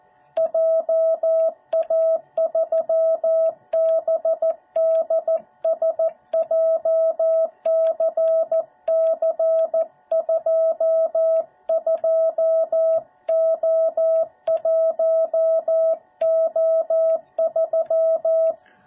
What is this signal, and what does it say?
Morse code, heard as a keyed tone, about 14 wpm, 650 Hz. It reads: JA3BDS JCC22O1O3